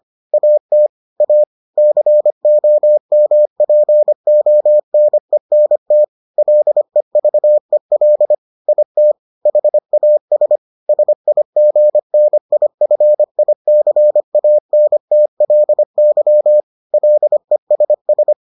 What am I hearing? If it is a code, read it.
Morse code, heard as a keyed tone, about 25 wpm, 600 Hz. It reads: AT A COMPONENT LEVEL IT HAS SIGNIFICANTLY LESS